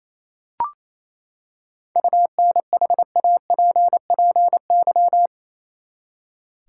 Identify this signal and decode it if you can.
Morse code, heard as a keyed tone, about 27 wpm, 700 Hz. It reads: UNHAPPY
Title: Seven Letter Words - 28wpm